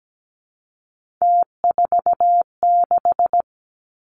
Morse code, heard as a keyed tone, about 17 wpm, 700 Hz. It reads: T46
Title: Three Challenging Letter-Number Combinations 17wpm